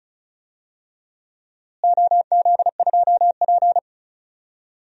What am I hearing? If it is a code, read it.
Morse code, heard as a keyed tone, about 34 wpm, 700 Hz. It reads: OZ2P
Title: DXCC-100 Call Signs - 4 Characters - Repeated 3x - 35wpm